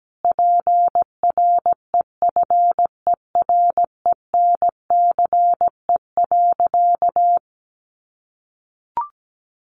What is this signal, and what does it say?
Morse code, heard as a keyed tone, about 17 wpm, 700 Hz. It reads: PREFERENCE.
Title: Sentences from Top 1500 Words 17wpm